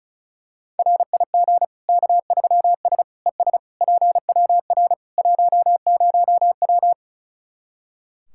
Morse code, heard as a keyed tone, about 34 wpm, 700 Hz. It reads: RIG K3S ES PWR 10W
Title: Practice QSO - Exchange 2 - Info - 35wpm